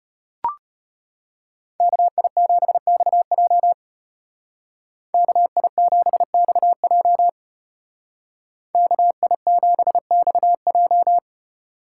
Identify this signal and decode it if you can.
Morse code, heard as a keyed tone, about 30 wpm, 700 Hz. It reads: KI7XJ KI7XJ KI7XJ
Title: Speed Racing - USA 2x2 Call Signs 25wpm